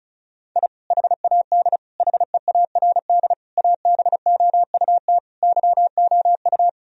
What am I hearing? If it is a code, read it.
Morse code, heard as a keyed tone, about 36 wpm, 700 Hz. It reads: I HAD HEARD ABOUT YOU
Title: Sentences from Top 1000 Words - Review Part 1 - 35wpm